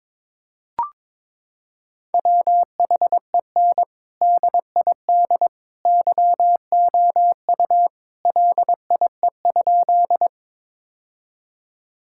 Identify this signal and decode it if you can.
Morse code, heard as a keyed tone, about 21 wpm, 700 Hz. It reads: WHEN DID YOU LIE?
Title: Sentences from Top 1600 Words 22wpm